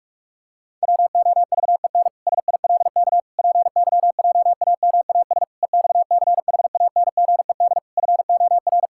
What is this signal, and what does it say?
Morse code, heard as a keyed tone, about 46 wpm, 700 Hz. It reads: WOVEN SILK PYJAMAS EXCHANGED FOR
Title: Pangram Challenge v2 - 45wpm